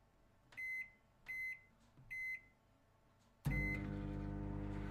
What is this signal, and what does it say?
There is no code.